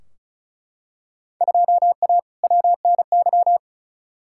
Morse code, heard as a keyed tone, about 36 wpm, 700 Hz. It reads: 2A WNY
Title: Field Day Exchanges - 35wpm